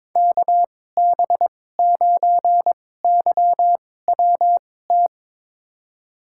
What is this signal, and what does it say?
Morse code, heard as a keyed tone, about 21 wpm, spaced as for 16 wpm, 700 Hz. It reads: KB9YWT
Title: USA Call Signs - 6 Characters - 2x Character Spacing - 22wpm